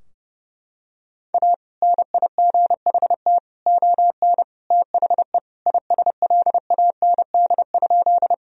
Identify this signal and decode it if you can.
Morse code, heard as a keyed tone, about 30 wpm, 700 Hz. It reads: A NIGHT ON THE ISLAND?
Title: Sentences from Top 1000 Words - Review Part 3 - 30wpm